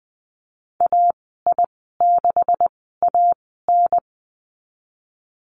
Morse code, heard as a keyed tone, about 20 wpm, spaced as for 14 wpm, 700 Hz. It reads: AI6AN
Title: 161 Lesson 26g - Letter g - Call Signs 20wpm